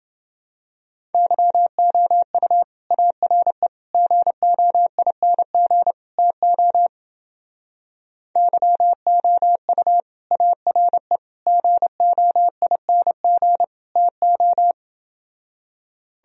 Morse code, heard as a keyed tone, about 29 wpm, 700 Hz. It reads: YOU ARE GOING TO YOU ARE GOING TO